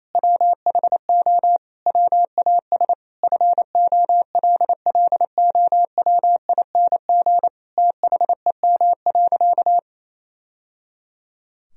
Morse code, heard as a keyed tone, about 27 wpm, 700 Hz. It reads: WHO WAS FOLLOWING THEM.
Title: Sentences from Top 1000 Words - Review Part 4 - 28wpm